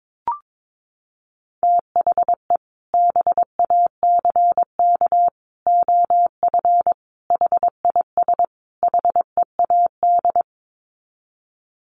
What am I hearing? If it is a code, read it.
Morse code, heard as a keyed tone, about 22 wpm, 700 Hz. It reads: THE BACK OF HIS HEAD
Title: Sets of 5 Words Encore 22wpm